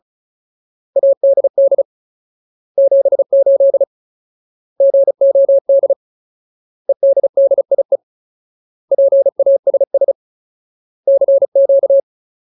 Morse code, heard as a keyed tone, about 35 wpm, 550 Hz. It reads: ADD 78 GOD EDDIE PASS CQ
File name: Ultimate Word List / 35WPM Ultimate